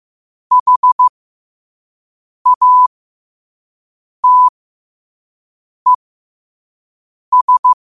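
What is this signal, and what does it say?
Morse code, heard as a keyed tone, about 15 wpm, spaced as for 5 wpm, 1000 Hz. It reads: HATES
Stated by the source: Morse code sent at 5 wpm.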